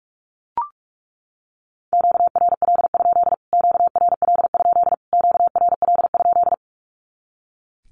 Morse code, heard as a keyed tone, about 44 wpm, 700 Hz. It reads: QRL? QRL? QRL?
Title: Q-Codes Repeated 3x 45wpm